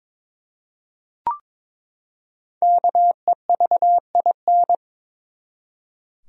Morse code, heard as a keyed tone, about 21 wpm, 700 Hz. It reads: KEVIN
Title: USA Common Names - Max 5 Letters - 22wpm